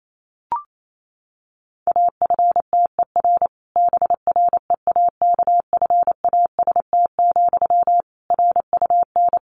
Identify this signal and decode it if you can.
Morse code, heard as a keyed tone, about 28 wpm, 700 Hz. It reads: AFTER BREAKFAST, RUN
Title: Sentences from Top 1500 Words 28wpm